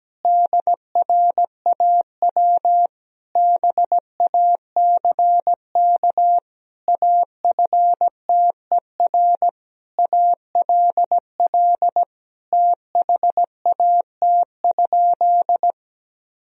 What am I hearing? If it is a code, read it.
Morse code, heard as a keyed tone, about 17 wpm, 700 Hz. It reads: DRAW BACK AFTER ALL THAT?
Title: Sentences from Top 1000 Words - Review Part 1 - 17wpm